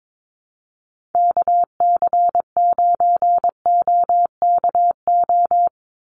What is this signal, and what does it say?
Morse code, heard as a keyed tone, about 22 wpm, 700 Hz. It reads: KC9OKO
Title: USA Call Signs 22wpm